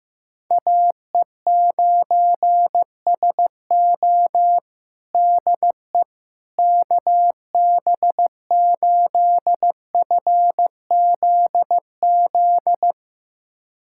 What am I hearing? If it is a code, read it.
Morse code, heard as a keyed tone, about 16 wpm, 700 Hz. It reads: AE9SO DE KB8FZZ
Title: USA Callsign DE Callsign 15wpm